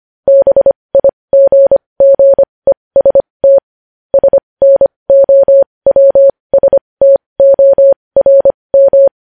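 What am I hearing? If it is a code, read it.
Morse code, heard as a keyed tone, about 25 wpm, spaced as for 20 wpm, 550 Hz. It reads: BIGGEST SNOWSTORM
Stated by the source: Morse news headlines sent at 20, opening with vvv.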